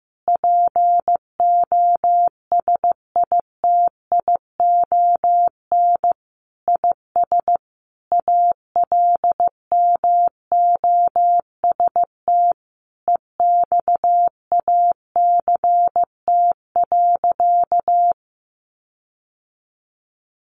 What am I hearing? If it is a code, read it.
Morse code, heard as a keyed tone, about 15 wpm, 700 Hz. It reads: POSITION IS ALMOST EXACT.
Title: Sentences from Top 1300 Words 15wpm